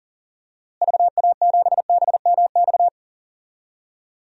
Morse code, heard as a keyed tone, about 40 wpm, 700 Hz. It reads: VA7BMX
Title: DXCC-100 Call Signs - 6 Characters - Repeated 3x - 40wpm